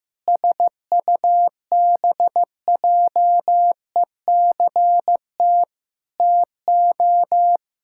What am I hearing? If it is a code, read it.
Morse code, heard as a keyed tone, about 15 wpm, 700 Hz. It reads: SUBJECT TO
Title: Pangram Challenge v2 - 15wpm